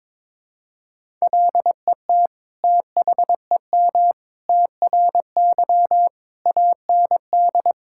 Morse code, heard as a keyed tone, about 22 wpm, 700 Hz. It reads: LET THEM TRY AND
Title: Sentences from Top 1000 Words - Review Part 5 - 22wpm